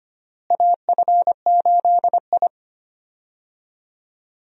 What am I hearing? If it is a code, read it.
Morse code, heard as a keyed tone, about 25 wpm, 700 Hz. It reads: AF8I
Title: USA Call Signs - 4 Characters - Repeated 3x - 25wpm